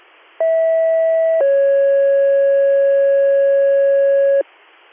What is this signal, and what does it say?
Two-tone page, heard as a one-second tone, then a three-second tone, about 650 then 550 Hz. It is heard from 0.4 to 4.4 s.